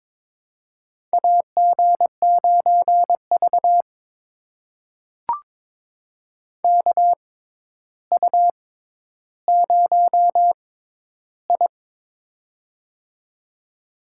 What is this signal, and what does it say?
Morse code, heard as a keyed tone, about 22 wpm, spaced as for 11 wpm, 700 Hz. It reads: AG9V KU0I
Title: USA Call Signs - 4 Characters - 6x Character Spacing - 22wpm